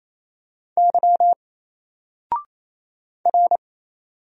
Morse code, heard as a keyed tone, about 28 wpm, 700 Hz. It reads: Y R
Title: Single Character (Letter:Number:Punctuation:Prosign) Alt 28wpm